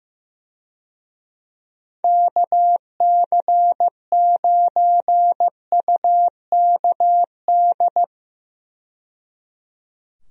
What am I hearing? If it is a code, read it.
Morse code, heard as a keyed tone, about 15 wpm, 700 Hz. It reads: KC9UKD
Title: USA Call Signs - 6 Characters - 15wpm